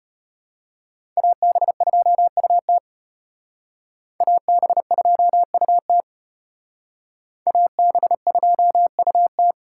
Morse code, heard as a keyed tone, about 33 wpm, 700 Hz. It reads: AB2UT AB2UT AB2UT